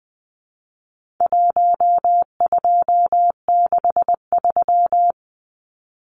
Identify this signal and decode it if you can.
Morse code, heard as a keyed tone, about 20 wpm, 700 Hz. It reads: 1263
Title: Numbers - 4-Digit 20wpm